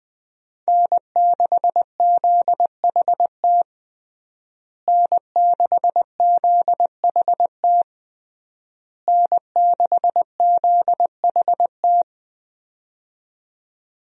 Morse code, heard as a keyed tone, about 20 wpm, 700 Hz. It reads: N6ZHT N6ZHT N6ZHT
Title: USA Call Signs - 5 Characters - Repeated 3x - 20wpm